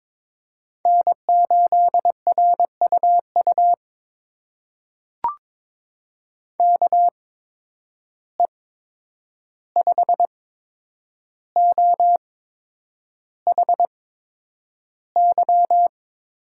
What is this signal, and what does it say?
Morse code, heard as a keyed tone, about 22 wpm, spaced as for 9 wpm, 700 Hz. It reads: N8RUU KE5OHY
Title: USA Call Signs - 8x Character Spacing - 22wpm